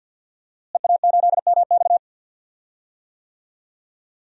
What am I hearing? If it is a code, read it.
Morse code, heard as a keyed tone, about 51 wpm, 700 Hz. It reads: EA8MX
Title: DXCC-100 Call Signs - 5 Characters - Repeated 3x - 50wpm